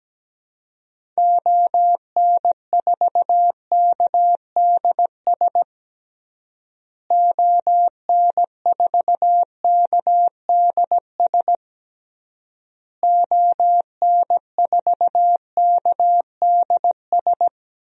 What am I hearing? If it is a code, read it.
Morse code, heard as a keyed tone, about 17 wpm, 700 Hz. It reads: ON4KDS ON4KDS ON4KDS
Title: DXCC-100 Call Signs - 6 Characters - Repeated 3x - 17wpm